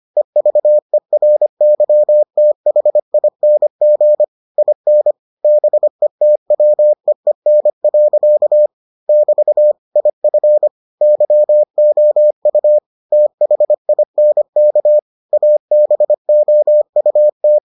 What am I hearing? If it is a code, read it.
Morse code, heard as a keyed tone, about 25 wpm, 600 Hz. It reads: EVERYTHING IN BETWEEN. <BT> IF YOU THINK ABOUT